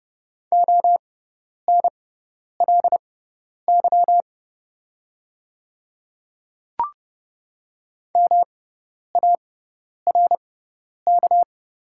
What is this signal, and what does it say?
Morse code, heard as a keyed tone, about 30 wpm, spaced as for 10 wpm, 700 Hz. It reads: ONLY MARK
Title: 6x Character Spacing - Top 300 Words - 30wpm